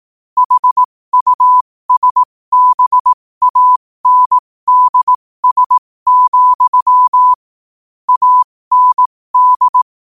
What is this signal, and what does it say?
Morse code, heard as a keyed tone, about 18 wpm, spaced as for 15 wpm, 1000 Hz. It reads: HUSBANDS, AND
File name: qotdfiles / 15wpm